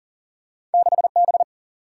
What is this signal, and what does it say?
Morse code, heard as a keyed tone, about 43 wpm, 700 Hz. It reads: 6 B
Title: Single-Letter Single-Number 40wpm